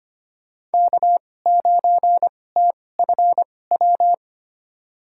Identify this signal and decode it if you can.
Morse code, heard as a keyed tone, about 25 wpm, spaced as for 18 wpm, 700 Hz. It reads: K9TFW
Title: USA Call Signs - 5 Characters - 2x Character Spacing - 25wpm